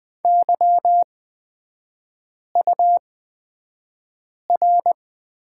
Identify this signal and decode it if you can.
Morse code, heard as a keyed tone, about 20 wpm, spaced as for 5 wpm, 700 Hz. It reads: YUR
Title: 250 Lesson 40e - BK - Three Characters 20wpm